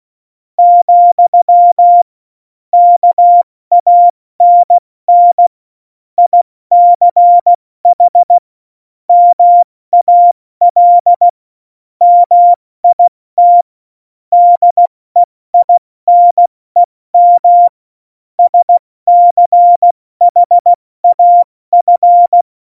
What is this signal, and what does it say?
Morse code, heard as a keyed tone, about 16 wpm, 700 Hz. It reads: , KANN ICH MAL MIT DEINEM SCHAF
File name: Morsekurs / Witze / Witz 06. Oktober 2020, 70 F80